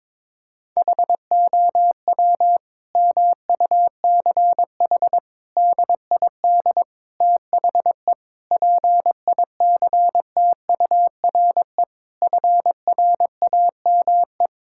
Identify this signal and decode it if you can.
Morse code, heard as a keyed tone, about 21 wpm, 700 Hz. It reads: HOW MUCH DID THE PICTURE FRAME
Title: Sentences from Top 1000 Words - Review Part 6 - 22wpm